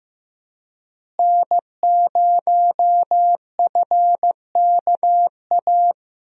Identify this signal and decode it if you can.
Morse code, heard as a keyed tone, about 16 wpm, 700 Hz. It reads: N0FKA
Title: USA Call Signs - 5 Characters - Repeated 3x - 15wpm